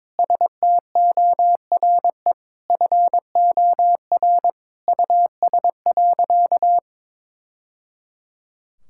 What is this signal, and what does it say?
Morse code, heard as a keyed tone, about 22 wpm, 700 Hz. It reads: STORE FOR US.
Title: Sentences from Top 1000 Words - Review Part 5 - 22wpm